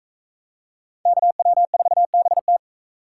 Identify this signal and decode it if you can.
Morse code, heard as a keyed tone, about 44 wpm, 700 Hz. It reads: KW4BT